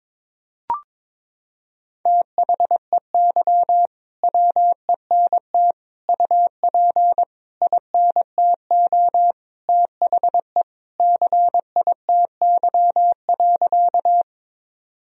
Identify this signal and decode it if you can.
Morse code, heard as a keyed tone, about 22 wpm, 700 Hz. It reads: THEY WENT UP INTO THE CITY.
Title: Sentences from Top 1000 Words - Review Part 1 - 22wpm